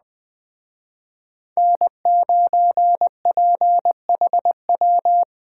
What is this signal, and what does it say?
Morse code, heard as a keyed tone, about 19 wpm, 700 Hz. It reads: N9PHW
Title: Speed Racing - USA 1x3 Call Signs 17wpm